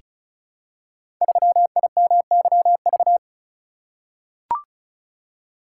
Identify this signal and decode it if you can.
Morse code, heard as a keyed tone, about 36 wpm, 700 Hz. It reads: 3IMYV
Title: Code Groups - 5 Letter:Numbers - 35wpm